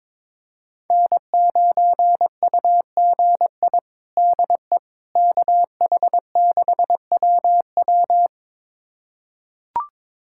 Morse code, heard as a keyed tone, about 22 wpm, 700 Hz. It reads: N9UGI DE KH6WW
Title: USA Callsign DE Callsign - 5 Characters - 22wpm